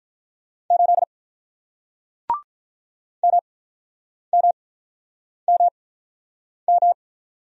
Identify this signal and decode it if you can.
Morse code, heard as a keyed tone, about 48 wpm, 700 Hz. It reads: 8 M M M M